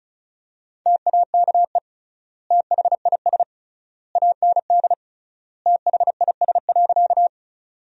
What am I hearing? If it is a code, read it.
Morse code, heard as a keyed tone, about 35 wpm, 700 Hz. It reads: TAKE THIS AND THIS.
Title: 3x Word-Spacing - Sentences from Top 200 Words - 35wpm